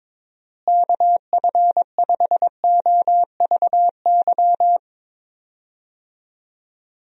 Morse code, heard as a keyed tone, about 21 wpm, 700 Hz. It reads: KF5OVY
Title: USA Call Signs - 6 Characters - Repeated 2x - 22wpm